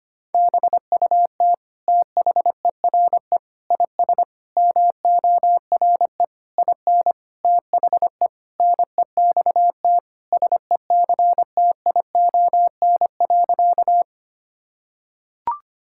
Morse code, heard as a keyed tone, about 25 wpm, 700 Hz. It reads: BUT THERE IS MORE IN THE NEXT SECTION.
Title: Sentences from Top 1000 Words - Review Part 5 - 25wpm